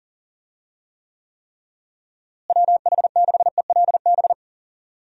Morse code, heard as a keyed tone, about 40 wpm, 700 Hz. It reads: WH6ELB
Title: USA Call Signs - 6 Characters - Repeated 3x - 40wpm